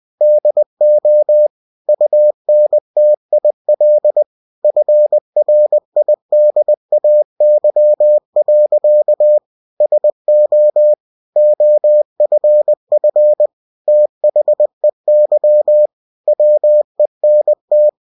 Morse code, heard as a keyed tone, about 20 wpm, 600 Hz. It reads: DO UNTIL FRIDAY. SO OFF THEY WENT
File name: cw-books / winnie-the-pooh / winnie-20wpm-page-0022